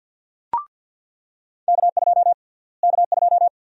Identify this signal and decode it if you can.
Morse code, heard as a keyed tone, about 47 wpm, 700 Hz. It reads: K2 K2
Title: USA Call Sign Prefixes - Repeated 2x - 50wpm